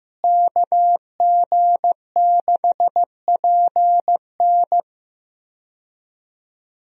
Morse code, heard as a keyed tone, about 15 wpm, 700 Hz. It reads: KG6PN
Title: USA Call Signs - 5 Characters - Repeated 3x - 15wpm